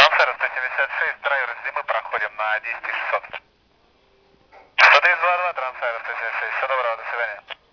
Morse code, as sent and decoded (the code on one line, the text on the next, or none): none